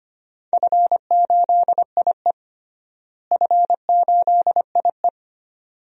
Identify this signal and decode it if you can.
Morse code, heard as a keyed tone, about 25 wpm, 700 Hz. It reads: F8IE F8IE
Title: DXCC-100 Call Signs - 4 Characters - Repeated 2x - 25wpm